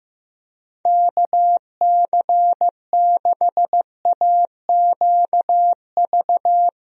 Morse code, heard as a keyed tone, about 15 wpm, 700 Hz. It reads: KC6AQV
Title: USA Call Signs - 6 Characters - Repeated 2x - 15wpm